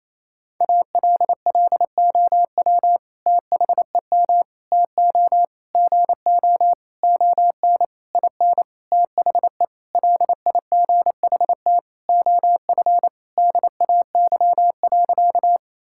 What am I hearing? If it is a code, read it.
Morse code, heard as a keyed tone, about 27 wpm, 700 Hz. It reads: ALLOW THEM TO GO ON IN THE LIGHT OF DAY.
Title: Sentences from Top 1000 Words - Review Part 5 - 28wpm